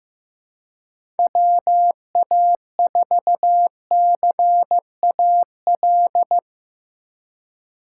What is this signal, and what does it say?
Morse code, heard as a keyed tone, about 17 wpm, 700 Hz. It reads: WA4CAL
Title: Speed Racing - USA 2x3 Call Signs 15wpm